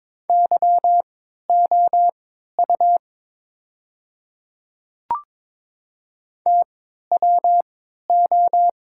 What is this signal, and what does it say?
Morse code, heard as a keyed tone, about 22 wpm, spaced as for 13 wpm, 700 Hz. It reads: YOU TWO
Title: 3x Character Spacing - Top 100 Words - 22wpm